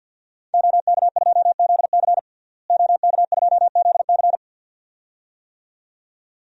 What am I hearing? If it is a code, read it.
Morse code, heard as a keyed tone, about 51 wpm, 700 Hz. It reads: OK2ZC OK2ZC
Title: DXCC-100 Call Signs - 5 Characters - Repeated 2x - 50wpm